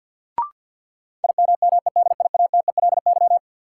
Morse code, heard as a keyed tone, about 53 wpm, 700 Hz. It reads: IMMEDIATELY
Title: Sentences from Top 1100 Words - New Word Drill Alt - 50wpm